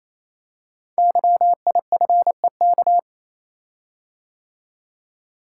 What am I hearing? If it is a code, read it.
Morse code, heard as a keyed tone, about 27 wpm, 700 Hz. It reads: YIFEK
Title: Code Groups - 5 Letters - 28wpm